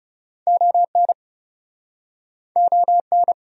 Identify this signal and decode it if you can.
Morse code, heard as a keyed tone, about 40 wpm, 700 Hz. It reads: ON ON